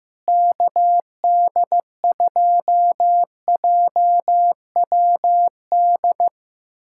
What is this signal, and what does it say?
Morse code, heard as a keyed tone, about 16 wpm, 700 Hz. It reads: KD2JWD
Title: USA Callsign DE Callsign 15wpm